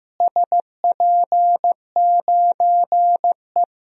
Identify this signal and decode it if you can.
Morse code, heard as a keyed tone, about 15 wpm, 700 Hz. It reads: SP9E
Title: DXCC-100 Call Signs - 4 Characters - Repeated 3x - 15wpm